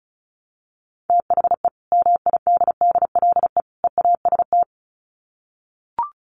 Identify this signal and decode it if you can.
Morse code, heard as a keyed tone, about 34 wpm, 700 Hz. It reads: THE MIDDLE EAST
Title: Sets of 3 Words 35wpm